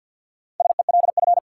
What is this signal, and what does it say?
Morse code, heard as a keyed tone, about 53 wpm, 700 Hz. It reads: SELF
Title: Sentences from Top 1400 Words - New Word Drill Alt - 50wpm